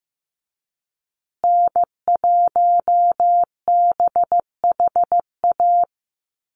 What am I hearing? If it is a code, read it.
Morse code, heard as a keyed tone, about 15 wpm, 700 Hz. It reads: N1BHA
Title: USA Call Signs Repeated 3x 15wpm